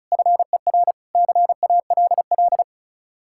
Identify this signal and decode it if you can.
Morse code, heard as a keyed tone, about 44 wpm, 700 Hz. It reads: FER CALL